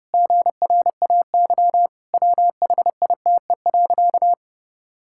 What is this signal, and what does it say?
Morse code, heard as a keyed tone, about 30 wpm, 700 Hz. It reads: GRAY WHITE.
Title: Pangram Challenge v2 - 30wpm